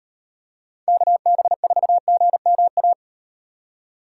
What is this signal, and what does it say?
Morse code, heard as a keyed tone, about 38 wpm, 700 Hz. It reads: KB4GMA